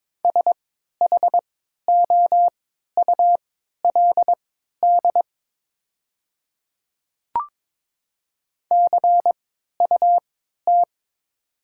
Morse code, heard as a keyed tone, about 22 wpm, spaced as for 12 wpm, 700 Hz. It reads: SHOULD CUT
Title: 3x Character Spacing - Top 500 Words - 22wpm